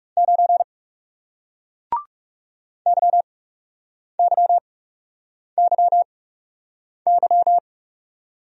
Morse code, heard as a keyed tone, about 38 wpm, spaced as for 8 wpm, 700 Hz. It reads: 9 YYYY